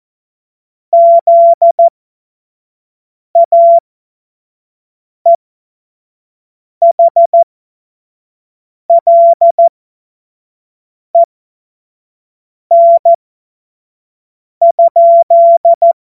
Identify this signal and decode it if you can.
Morse code, heard as a keyed tone, about 14 wpm, spaced as for 5 wpm, 680 Hz. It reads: ZAEHLEN?
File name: Morsekurs / Witze / Witz 10 Juni 2025 25 F70